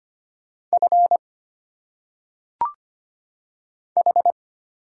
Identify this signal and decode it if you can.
Morse code, heard as a keyed tone, about 25 wpm, 700 Hz. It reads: F H
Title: Single Letters - Phonetic Pronunciation 25wpm